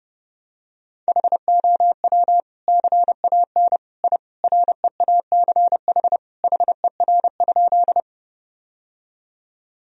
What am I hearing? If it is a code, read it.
Morse code, heard as a keyed tone, about 30 wpm, 700 Hz. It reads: HOW CAN I REACH HER?
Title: Sentences from Top 1000 Words - Review Part 2 - 30wpm